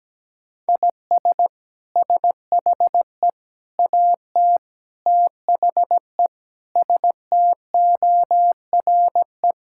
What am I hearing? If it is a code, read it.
Morse code, heard as a keyed tone, about 17 wpm, 700 Hz. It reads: IS SHE AT THE STORE
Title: Sentences from Top 1000 Words - Review Part 5 - 17wpm